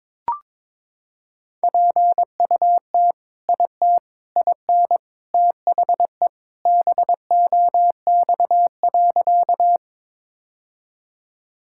Morse code, heard as a keyed tone, about 21 wpm, 700 Hz. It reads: PUT IT IN THE BOX.
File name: practice / 22wpm / Sentences with 3-Word Phrases - 22wpm